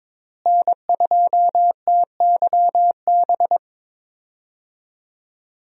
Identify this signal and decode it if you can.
Morse code, heard as a keyed tone, about 22 wpm, 700 Hz. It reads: N2TYB
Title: USA Call Signs - 5 Characters - Repeated 3x - 22wpm